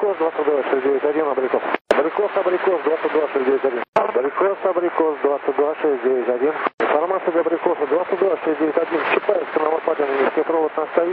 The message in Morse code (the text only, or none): none